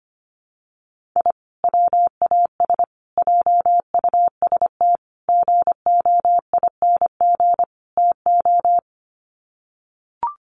Morse code, heard as a keyed tone, about 25 wpm, 700 Hz. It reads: I WAS JUST GOING TO
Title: Sets of 5 Words Encore 25wpm